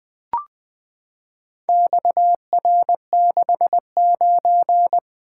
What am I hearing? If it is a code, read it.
Morse code, heard as a keyed tone, about 20 wpm, 700 Hz. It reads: XR69
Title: Four Letter-Number Combinations - 20wpm